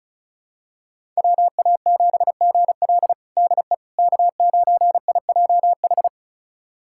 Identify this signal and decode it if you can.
Morse code, heard as a keyed tone, about 36 wpm, 700 Hz. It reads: WA7GL DE K9IJH
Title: USA Callsign DE Callsign - 5 Characters - 35wpm